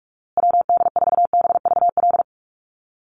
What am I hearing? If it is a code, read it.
Morse code, heard as a keyed tone, about 46 wpm, 700 Hz. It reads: WD4BVL
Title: USA Call Signs Repeated 2x 45wpm